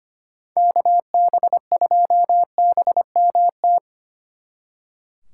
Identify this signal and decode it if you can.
Morse code, heard as a keyed tone, about 31 wpm, 700 Hz. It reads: KB2BMT
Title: Speed Racing - USA 2x3 Call Signs 25wpm